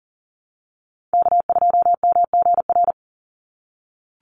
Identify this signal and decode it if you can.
Morse code, heard as a keyed tone, about 40 wpm, 700 Hz. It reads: K2MGR
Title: Call Signs - 40wpm